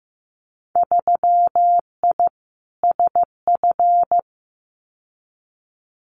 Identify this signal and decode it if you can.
Morse code, heard as a keyed tone, about 15 wpm, 700 Hz. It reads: 3I SF
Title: Winter Field Day Exchanges 15wpm